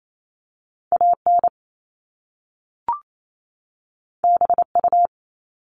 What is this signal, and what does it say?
Morse code, heard as a keyed tone, about 28 wpm, 700 Hz. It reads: AN BU
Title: Top Two Letter Combinations in Words 28wpm